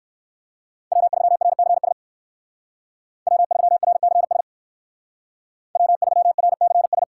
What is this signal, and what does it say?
Morse code, heard as a keyed tone, about 59 wpm, 690 Hz. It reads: W3RCS W3RCS W3RCS